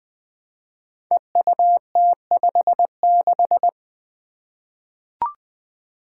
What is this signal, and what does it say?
Morse code, heard as a keyed tone, about 20 wpm, 700 Hz. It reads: EUT56
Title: Code Groups - 5 Letter:Numbers - 20wpm